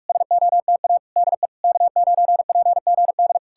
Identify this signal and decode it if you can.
Morse code, heard as a keyed tone, about 59 wpm, spaced as for 52 wpm, 700 Hz. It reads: SOTA DE K9PGD
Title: Speed-Racing Common QSO Phrases - 45wpm